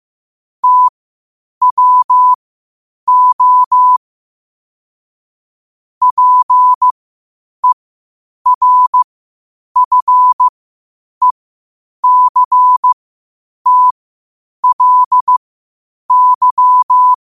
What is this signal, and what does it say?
Morse code, heard as a keyed tone, about 15 wpm, spaced as for 8 wpm, 1000 Hz. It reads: TWO PERFECTLY
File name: qotdfiles / 8wpm